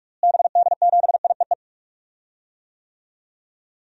Morse code, heard as a keyed tone, about 46 wpm, 700 Hz. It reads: BD7IEE
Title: DXCC-100 Call Signs - 6 Characters - Repeated 2x - 45wpm